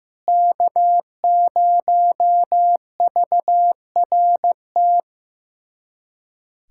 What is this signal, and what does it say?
Morse code, heard as a keyed tone, about 15 wpm, 700 Hz. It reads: K0VRT
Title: USA Callsign DE Callsign - 5 Characters - 15wpm